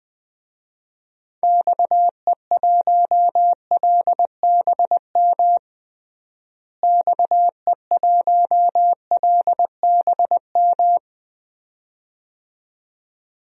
Morse code, heard as a keyed tone, about 20 wpm, 700 Hz. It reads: XE1LBM XE1LBM
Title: DXCC-100 Call Signs - 6 Characters - Repeated 2x - 20wpm